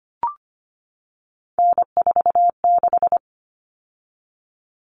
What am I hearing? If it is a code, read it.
Morse code, heard as a keyed tone, about 25 wpm, 700 Hz. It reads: N46
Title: Common Cut Numbers - 3-Digit 25wpm